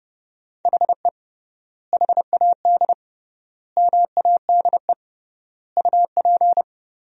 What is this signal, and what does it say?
Morse code, heard as a keyed tone, about 30 wpm, 700 Hz. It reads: HE HAD MADE UP
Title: 3x Word-Spacing - Sentences from Top 500 Words - 30wpm